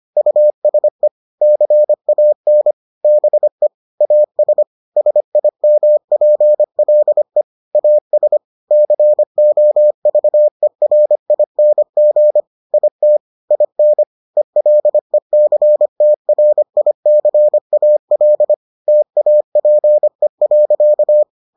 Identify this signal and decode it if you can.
Morse code, heard as a keyed tone, about 25 wpm, 600 Hz. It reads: USE CAN BE AS SIMPLE AS COVERING IT IN ELECTRICAL TAPE.